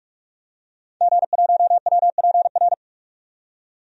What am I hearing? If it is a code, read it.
Morse code, heard as a keyed tone, about 46 wpm, 700 Hz. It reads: G1WPR
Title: DXCC-100 Call Signs - 5 Characters - 45wpm